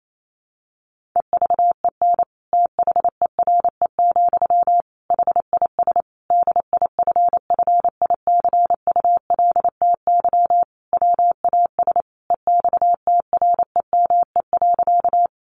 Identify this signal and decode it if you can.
Morse code, heard as a keyed tone, about 28 wpm, 700 Hz. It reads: EVEN THERE, HIS DIFFICULTY WAS EXTREME.
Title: Sentences from Top 1500 Words 28wpm